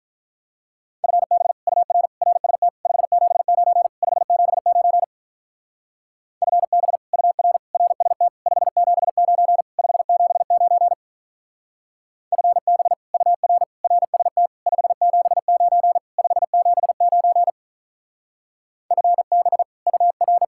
Morse code, heard as a keyed tone, about 43 wpm, 700 Hz. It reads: FB UR RST 579 579 FB UR RST 579 579 FB UR RST 579 579 FB UR
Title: Speed-Racing Common QSO Phrases - 35wpm